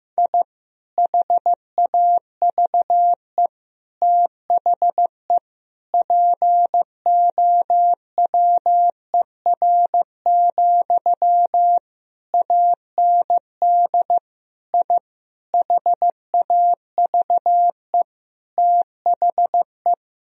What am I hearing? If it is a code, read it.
Morse code, heard as a keyed tone, about 15 wpm, 700 Hz. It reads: I HAVE THE POWER, AND I HAVE THE
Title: Sentences from Top 1000 Words - Review Part 2 - 15wpm